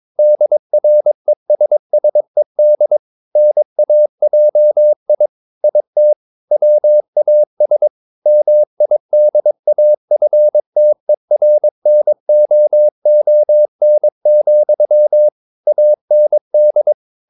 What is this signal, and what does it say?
Morse code, heard as a keyed tone, about 23 wpm, 600 Hz. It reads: DRESSED NAJI IT WAS MIDAFTERNOON, AND
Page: Romanen Never af Ken Follet på engelsk.